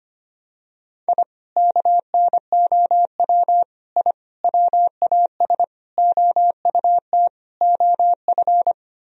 Morse code, heard as a keyed tone, about 25 wpm, 700 Hz. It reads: I KNOW I WAS OUT OF
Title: Sentences from Top 1000 Words - Review Part 2 - 25wpm